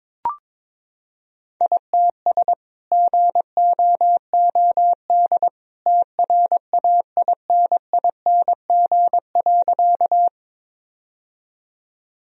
Morse code, heard as a keyed tone, about 22 wpm, 700 Hz. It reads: ITS GOOD TRAINING.
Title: Sentences from Top 1000 Words - Review Part 4 - 22wpm